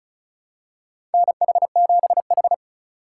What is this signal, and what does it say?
Morse code, heard as a keyed tone, about 34 wpm, 700 Hz. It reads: NH7H
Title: USA Call Signs - 4 Characters - 4x Character Spacing - 35wpm